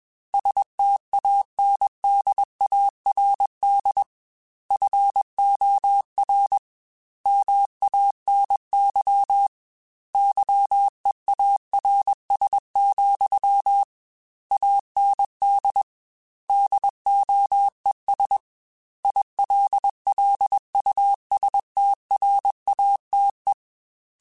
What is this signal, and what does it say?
Morse code, heard as a keyed tone, about 21 wpm, 800 Hz. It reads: STANDARD FOR MANY YEARS, AND DOES ILLUSTRATE
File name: Technical Subjects / Technical Subjects-mp3s / AND Gate @20